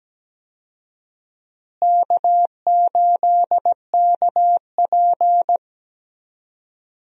Morse code, heard as a keyed tone, about 17 wpm, 700 Hz. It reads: K8KP
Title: USA Call Signs - 4 Characters - Repeated 3x - 17wpm